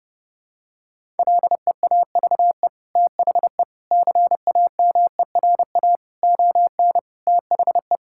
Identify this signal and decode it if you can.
Morse code, heard as a keyed tone, about 30 wpm, 700 Hz. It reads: LEAVE THE CAMERA ON THE
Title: Sentences from Top 1000 Words - Review Part 5 - 30wpm